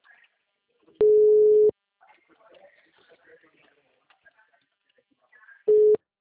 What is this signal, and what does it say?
Ringback tone (Europe, 425 Hz), with speech.